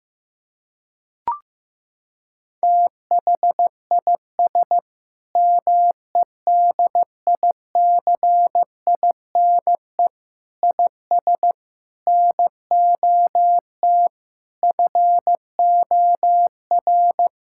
Morse code, heard as a keyed tone, about 15 wpm, 700 Hz. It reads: THIS MEDICINE IS NOT FOR
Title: Sentences from Top 1000 Words - Review Part 6 - 15wpm